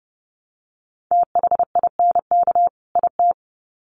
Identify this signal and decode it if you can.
Morse code, heard as a keyed tone, about 30 wpm, 700 Hz. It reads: THINK IT
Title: Sets of 2 Words 30wpm